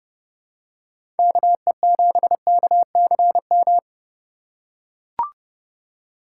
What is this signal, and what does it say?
Morse code, heard as a keyed tone, about 31 wpm, 700 Hz. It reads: KE7KCM
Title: USA Call Signs - 3x Character Spacing - 30wpm